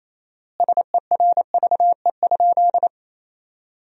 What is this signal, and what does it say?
Morse code, heard as a keyed tone, about 28 wpm, 700 Hz. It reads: SERVE?
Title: 4x Word-Spacing - Sentences from Top 400 Words - 28wpm